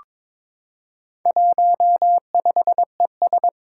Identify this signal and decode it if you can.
Morse code, heard as a keyed tone, about 22 wpm, 700 Hz. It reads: 15ES
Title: Four Challenging Letter-Number Combinations 22wpm